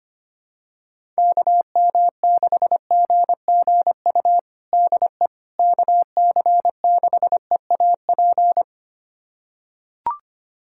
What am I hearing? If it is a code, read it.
Morse code, heard as a keyed tone, about 25 wpm, 700 Hz. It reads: KM6GGU DE KC6EAP
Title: USA Callsign DE Callsign 25wpm